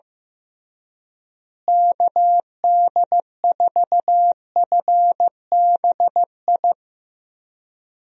Morse code, heard as a keyed tone, about 15 wpm, 700 Hz. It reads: KD4FBI
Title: USA Call Signs - 6 Characters - Repeated 3x - 15wpm